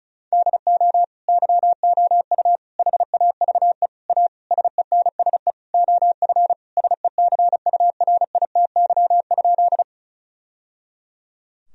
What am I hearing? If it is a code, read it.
Morse code, heard as a keyed tone, about 36 wpm, 700 Hz. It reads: DO YOU HAVE A SENSE OF SECURITY?
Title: Sentences from Top 1000 Words - Review Part 4 - 35wpm